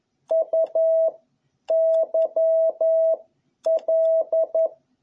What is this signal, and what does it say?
Morse code, heard as a keyed tone, about 11 wpm, 650 Hz. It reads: U Y L
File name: jh8cbh / 2021cw